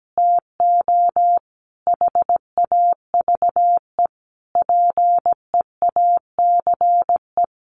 Morse code, heard as a keyed tone, about 17 wpm, 700 Hz. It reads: TO HAVE PEACE
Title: Sentences from Top 1500 Words 17wpm